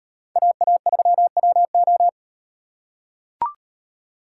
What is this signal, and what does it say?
Morse code, heard as a keyed tone, about 38 wpm, 700 Hz. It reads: AA3WO